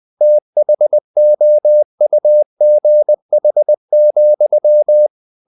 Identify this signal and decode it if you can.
Morse code, heard as a keyed tone, about 20 wpm, 600 Hz. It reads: THOUGH,
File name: the old man and the sea / the old man and the sea part 1 20wpm